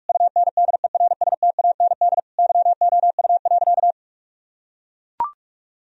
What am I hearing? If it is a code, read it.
Morse code, heard as a keyed tone, about 41 wpm, 700 Hz. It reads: UNDERSTAND YOU.
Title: Sentences from Top 1000 Words - Review Part 2 - 45wpm